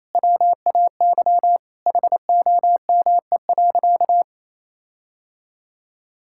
Morse code, heard as a keyed tone, about 28 wpm, 700 Hz. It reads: WAY HOME.
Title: Sentences from Top 1600 Words 28wpm